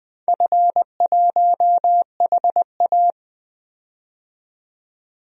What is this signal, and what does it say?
Morse code, heard as a keyed tone, about 20 wpm, 700 Hz. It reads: F1HA
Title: DXCC-100 Call Signs - 4 Characters - 20wpm